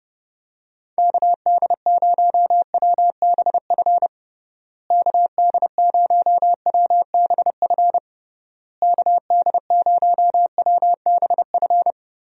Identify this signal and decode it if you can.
Morse code, heard as a keyed tone, about 30 wpm, 700 Hz. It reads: KD0WBF KD0WBF KD0WBF
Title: USA Call Signs - 6 Characters - Repeated 3x - 30wpm